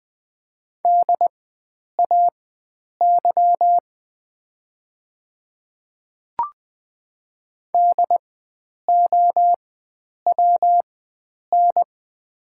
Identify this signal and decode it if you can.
Morse code, heard as a keyed tone, about 20 wpm, spaced as for 9 wpm, 700 Hz. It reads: DAY DOWN
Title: 4x Character Spacing - Top 100 Words - 20wpm